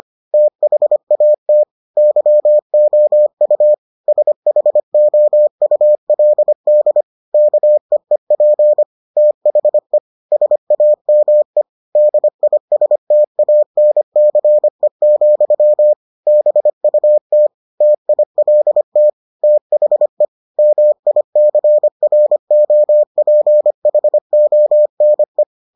Morse code, heard as a keyed tone, about 25 wpm, 600 Hz. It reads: THAT YOU SHOULD KEEP THE SAME DISTANCE, BUT TILT THE MICROPHONE